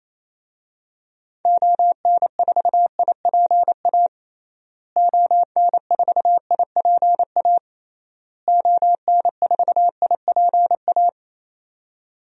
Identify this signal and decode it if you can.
Morse code, heard as a keyed tone, about 28 wpm, 700 Hz. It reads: ON4IPA ON4IPA ON4IPA
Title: DXCC-100 Call Signs - 6 Characters - Repeated 3x - 28wpm